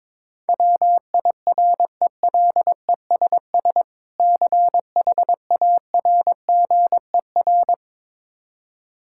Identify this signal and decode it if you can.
Morse code, heard as a keyed tone, about 22 wpm, 700 Hz. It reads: WIRELESS CHARGER
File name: Difficult Words with Hints - 22wpm